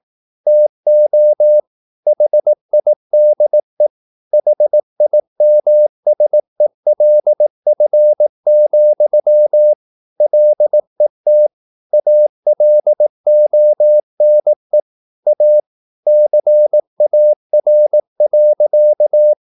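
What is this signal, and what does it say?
Morse code, heard as a keyed tone, about 18 wpm, 600 Hz. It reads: TO HIDE HIMSELF, LET ALONE A CAR.
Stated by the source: Morse practice at 18 wpm, English text, from CW med Gnister.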